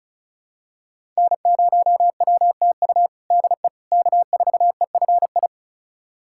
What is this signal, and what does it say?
Morse code, heard as a keyed tone, about 36 wpm, 690 Hz. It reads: N0WTU DE K4EFI
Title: USA Callsign DE Callsign - 5 Characters - 35wpm